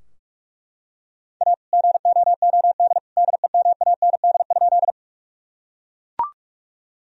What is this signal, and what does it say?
Morse code, heard as a keyed tone, about 41 wpm, 700 Hz. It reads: A GOOD DEMAND?
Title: Sentences from Top 1000 Words - Review Part 4 - 45wpm